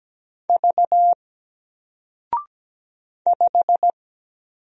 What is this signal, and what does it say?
Morse code, heard as a keyed tone, about 17 wpm, 700 Hz. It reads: V 5
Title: Single Letter-Number 17wpm